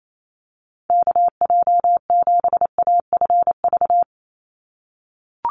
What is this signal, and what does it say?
Morse code, heard as a keyed tone, about 28 wpm, 700 Hz. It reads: KJ7AFV
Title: USA Call Signs 28wpm